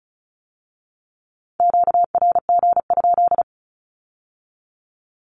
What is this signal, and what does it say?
Morse code, heard as a keyed tone, about 36 wpm, 700 Hz. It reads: QRG?
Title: Q-Codes with Spoken Meaning - 35wpm